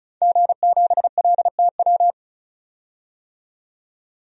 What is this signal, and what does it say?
Morse code, heard as a keyed tone, about 34 wpm, 700 Hz. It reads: G7LTW
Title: DXCC-100 Call Signs - 5 Characters - 35wpm